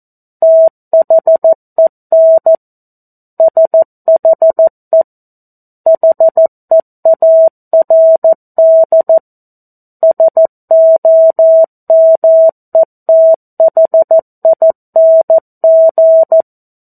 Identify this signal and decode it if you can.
Morse code, heard as a keyed tone, about 14 wpm, 650 Hz. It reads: THEN SHE HEARD SOMETHING
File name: 02-Chapter-02 14 WPM